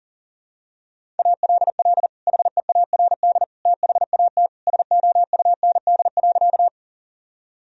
Morse code, heard as a keyed tone, about 37 wpm, 700 Hz. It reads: ALL HEARD THAT SOUND.
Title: Sentences from Top 1000 Words - Review Part 1 - 40wpm